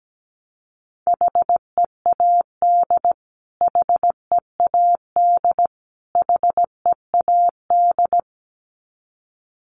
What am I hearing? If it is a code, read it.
Morse code, heard as a keyed tone, about 17 wpm, 700 Hz. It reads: HEAD HEAD HEAD
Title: Top 500 Words Repeated 17wpm